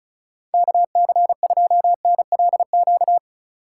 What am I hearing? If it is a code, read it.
Morse code, heard as a keyed tone, about 43 wpm, 700 Hz. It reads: KC2NLQ
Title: Speed Racing - USA 2x3 Call Signs 35wpm